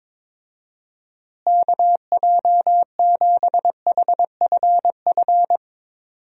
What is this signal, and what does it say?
Morse code, heard as a keyed tone, about 21 wpm, 700 Hz. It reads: KJ7HFF
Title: USA Call Signs - 6 Characters - Repeated 3x - 22wpm